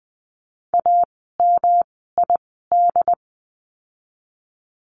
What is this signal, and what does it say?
Morse code, heard as a keyed tone, about 19 wpm, spaced as for 14 wpm, 700 Hz. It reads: AMID
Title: 108 Lesson 18f - Letter m - Words 20wpm